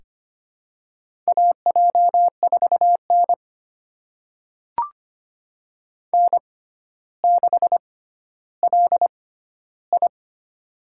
Morse code, heard as a keyed tone, about 25 wpm, spaced as for 13 wpm, 700 Hz. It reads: AJ4N N6LI
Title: USA Call Signs - 4 Characters - 6x Character Spacing - 25wpm